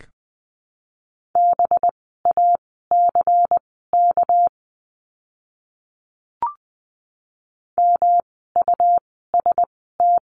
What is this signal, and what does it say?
Morse code, heard as a keyed tone, about 20 wpm, spaced as for 15 wpm, 700 Hz. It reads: BACK MUST
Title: 237 Lesson 38f - Letter q - Words 20wpm